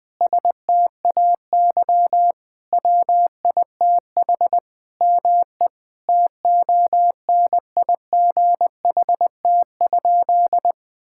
Morse code, heard as a keyed tone, about 20 wpm, 700 Hz. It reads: STAY WITH ME TONIGHT?
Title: Sentences from Top 1600 Words 20wpm